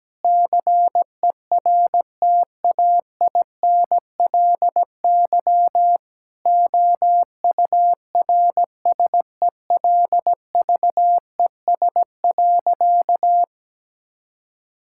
Morse code, heard as a keyed tone, about 17 wpm, 700 Hz. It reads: CERTAINLY OURSELVES.
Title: Sentences from Top 1600 Words 17wpm